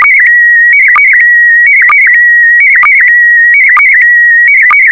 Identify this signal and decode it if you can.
SSTV (slow-scan television), PD 290.
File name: sstv / PD290-VK4MSL